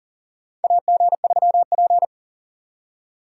Morse code, heard as a keyed tone, about 40 wpm, 700 Hz. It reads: AG3P
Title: USA Call Signs - 4 Characters - 2x Character Spacing - 40wpm